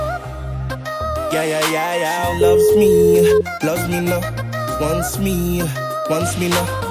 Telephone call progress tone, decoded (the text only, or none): ringback tone